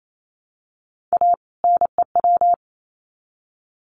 Morse code, heard as a keyed tone, about 28 wpm, 700 Hz. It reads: A NEW
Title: Sets of 2 Words 28wpm